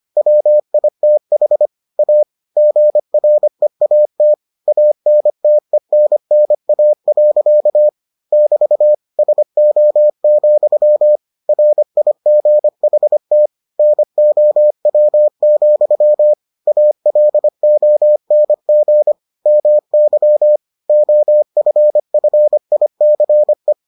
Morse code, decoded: WITH A GREAT ANTENNA. <BT> SO, RIGHT NOW, ALONG MY OFFICE

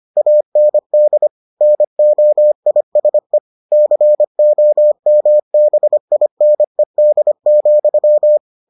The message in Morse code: AND NOISE COMBINED,

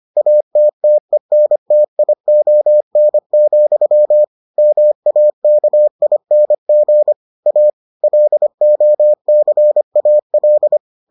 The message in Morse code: ATTENTION, MAKING A LOCAL

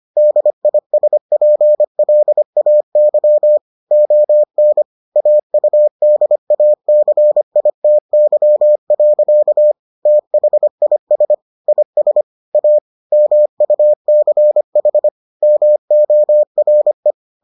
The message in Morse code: DISPLAY ON AUDACITY. THIS IS A MUCH MORE

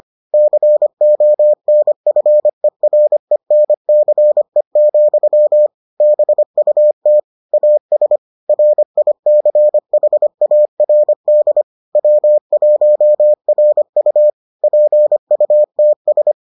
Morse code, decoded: CONFERENCE, BUT AS RICHARD W1RU PUTS

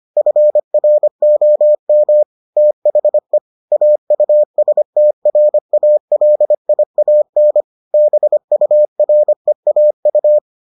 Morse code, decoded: FROM THE AUSTRALIAN BUREAU

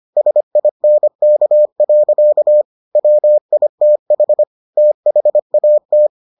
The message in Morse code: SINK. WITH THAT

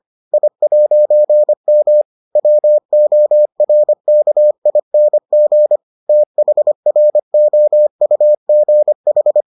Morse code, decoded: I'M WORKING THROUGH

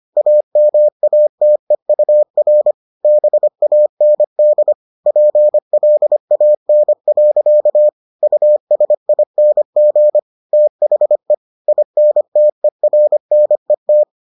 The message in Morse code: AMATEUR BAND PLAN. USING THE INTERNET

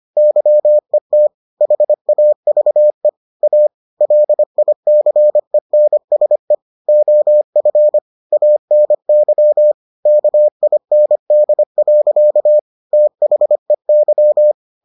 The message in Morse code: YET HAVE A LICENSE OF ANY KIND. THEY

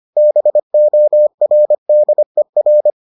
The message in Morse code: BORDER

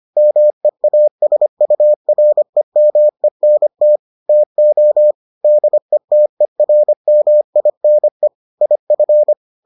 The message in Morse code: MEASUREMENT TO DETERMINE IF

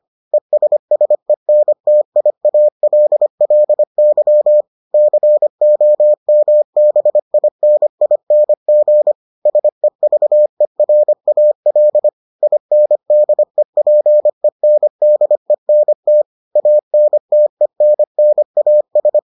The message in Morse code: ESSENTIALLY COMBINING SEVERAL INDEPENDENT ANTENNAS

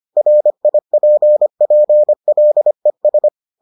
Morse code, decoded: RIPPLES

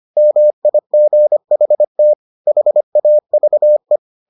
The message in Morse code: MIGHT HAVE

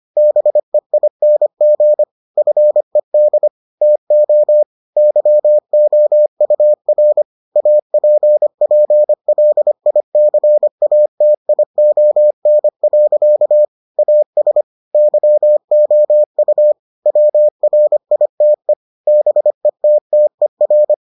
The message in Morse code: BEING FED TO YOUR APPLICATION. AS YOU WRITE BETTER

